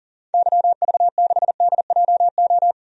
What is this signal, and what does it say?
Morse code, heard as a keyed tone, about 40 wpm, 700 Hz. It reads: YV6DJO
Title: DXCC-100 Call Signs - 6 Characters - 40wpm